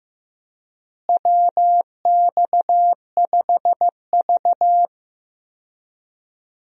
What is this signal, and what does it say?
Morse code, heard as a keyed tone, about 15 wpm, 700 Hz. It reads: WX5V
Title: USA Call Signs - 4 Characters - Repeated 2x - 15wpm